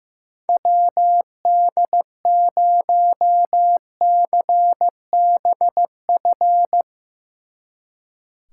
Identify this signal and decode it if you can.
Morse code, heard as a keyed tone, about 16 wpm, 700 Hz. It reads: WD0CBF
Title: Speed Racing - USA Call Signs 15wpm